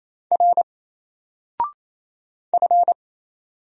Morse code, heard as a keyed tone, about 27 wpm, 700 Hz. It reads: R F
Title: Single Character (Letter:Number:Punctuation:Prosign) 28wpm